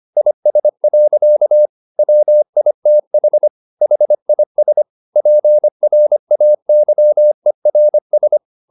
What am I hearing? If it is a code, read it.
Morse code, heard as a keyed tone, about 25 wpm, 600 Hz. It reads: IS. WITH HIS PRAYERS
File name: the old man and the sea part 1 25wpm